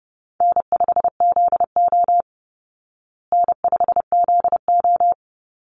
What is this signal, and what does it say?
Morse code, heard as a keyed tone, about 30 wpm, 700 Hz. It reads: N5ZO N5ZO
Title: Easy Call Signs Repeated 2x - 30wpm